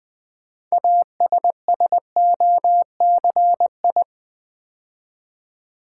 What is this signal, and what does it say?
Morse code, heard as a keyed tone, about 20 wpm, 700 Hz. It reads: ASSOCI
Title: Build A Word - 3 Letters At A Time - 9-Letter Words - 20wpm